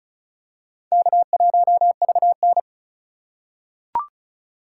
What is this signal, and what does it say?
Morse code, heard as a keyed tone, about 36 wpm, 700 Hz. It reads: K1VN
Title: USA Call Signs - 4 Characters - 3x Character Spacing - 35wpm